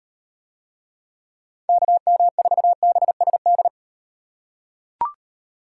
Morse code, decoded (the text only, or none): KM4BSD